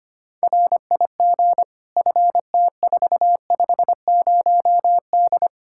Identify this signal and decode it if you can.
Morse code, heard as a keyed tone, about 25 wpm, 700 Hz. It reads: RIG FT450D
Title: K1USN SST QSO Practice 25wpm